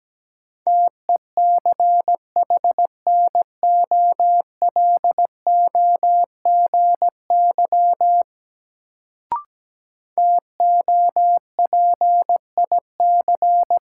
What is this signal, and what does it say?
Morse code, heard as a keyed tone, about 17 wpm, 700 Hz. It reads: TECHNOLOGY TOPIC
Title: Sentences from Top 1200 Words - New Word Drill Alt - 17wpm